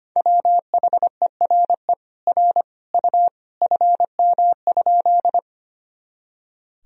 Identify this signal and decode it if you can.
Morse code, heard as a keyed tone, about 25 wpm, 700 Hz. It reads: WHERE R U FM?
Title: Sentences from Top 200 Words and Abbreviations 25wpm